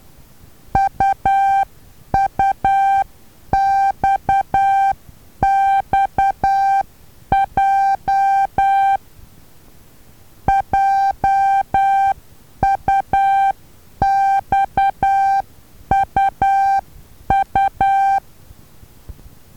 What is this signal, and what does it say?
Morse code, heard as a keyed tone, about 10 wpm, 780 Hz. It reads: UUXXJ JUXUU
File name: Taviro HA6VA B